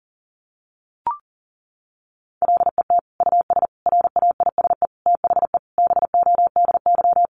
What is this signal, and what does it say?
Morse code, heard as a keyed tone, about 40 wpm, 700 Hz. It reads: LET US RAISE THE BODY
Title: Sentences from Top 1400 Words 40wpm